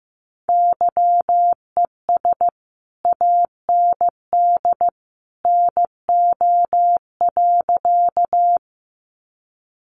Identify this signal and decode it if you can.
Morse code, heard as a keyed tone, about 14 wpm, 700 Hz. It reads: YES AND NO.
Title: Sentences from Top 500 Words 15wpm